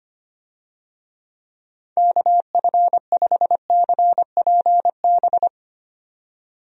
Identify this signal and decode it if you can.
Morse code, heard as a keyed tone, about 25 wpm, 700 Hz. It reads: KF5CPB
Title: USA Call Signs - 6 Characters - 25wpm